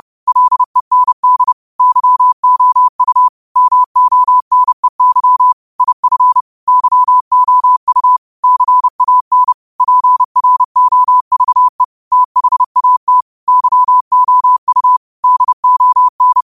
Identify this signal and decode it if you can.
Morse code, heard as a keyed tone, about 30 wpm, 1000 Hz. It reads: LEND YOU MONEY IF YOU CAN PROVE THAT YOU DON
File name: qotdfiles / 30wpm